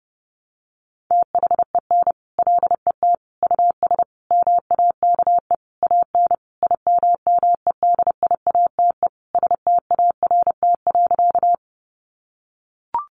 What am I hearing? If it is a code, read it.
Morse code, heard as a keyed tone, about 30 wpm, 700 Hz. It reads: THEN LET US MAKE AN IMMEDIATE START.
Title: Sentences from Top 1200 Words 30wpm